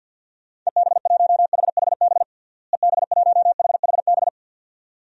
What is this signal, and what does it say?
Morse code, heard as a keyed tone, about 51 wpm, 700 Hz. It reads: EB1HHB EB1HHB
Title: DXCC-100 Call Signs - 6 Characters - Repeated 2x - 50wpm